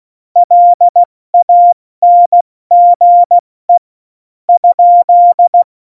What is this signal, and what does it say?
Morse code, heard as a keyed tone, about 16 wpm, 700 Hz. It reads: LANGE ?
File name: Morsekurs / Witze / Witz 13 March 2018, 70 F80